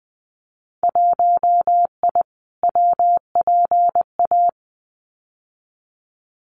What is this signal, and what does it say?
Morse code, heard as a keyed tone, about 19 wpm, 700 Hz. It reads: 1I WPA
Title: Winter Field Day Exchanges Encore 20wpm